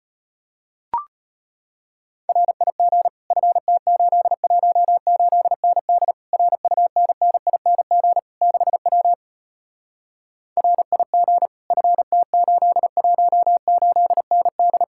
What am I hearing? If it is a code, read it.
Morse code, heard as a keyed tone, about 33 wpm, 700 Hz. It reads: RIG FT818ND RUNNING 6W RIG FT818ND